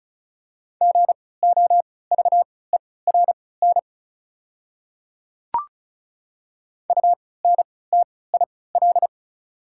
Morse code, heard as a keyed tone, about 36 wpm, spaced as for 20 wpm, 690 Hz. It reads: GOVERN UNTIL
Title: 3x Character Spacing - Top 500 Words - 35wpm